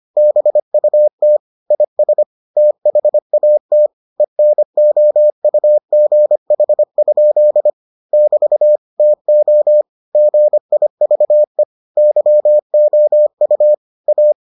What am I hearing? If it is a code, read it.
Morse code, heard as a keyed tone, about 25 wpm, 600 Hz. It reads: BUT IS THAT ENOUGH? <BT> TO GIVE YOU A